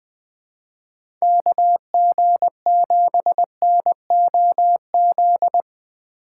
Morse code, heard as a keyed tone, about 20 wpm, 700 Hz. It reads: KG7NOZ
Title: USA Call Signs - 6 Characters - 3x Character Spacing - 20wpm